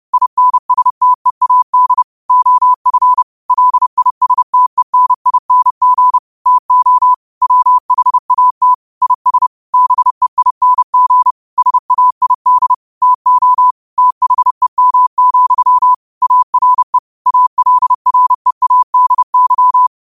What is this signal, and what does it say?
Morse code, heard as a keyed tone, about 30 wpm, 1000 Hz. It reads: INSTEAD OF LISTENING TO WHAT IS BEING SAID TO THEM, ARE ALREADY
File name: qotdfiles / 30wpm